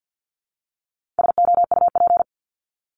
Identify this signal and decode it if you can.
Morse code, heard as a keyed tone, about 48 wpm, 700 Hz. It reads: SOUP
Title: Top 751-1000 Words 50wpm